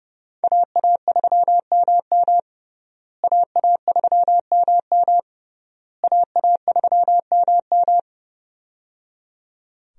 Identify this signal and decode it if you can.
Morse code, heard as a keyed tone, about 30 wpm, 700 Hz. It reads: AA3MM AA3MM AA3MM
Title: USA Call Signs - 5 Characters - Repeated 3x - 30wpm